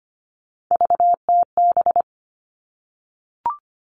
Morse code, heard as a keyed tone, about 25 wpm, 700 Hz. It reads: VTB
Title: All Cut Numbers - 3-Digit 25wpm